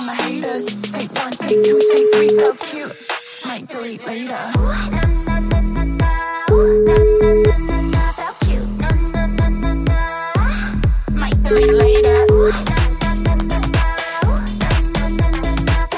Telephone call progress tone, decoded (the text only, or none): ringback tone